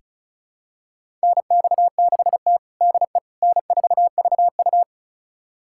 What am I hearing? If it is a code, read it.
Morse code, heard as a keyed tone, about 36 wpm, 700 Hz. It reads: NX6T DE N4VU
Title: USA Callsign DE Callsign - 4 Characters - 35wpm